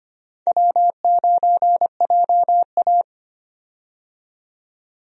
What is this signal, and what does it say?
Morse code, heard as a keyed tone, about 25 wpm, 700 Hz. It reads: W9JA
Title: USA Call Signs - 4 Characters - 25wpm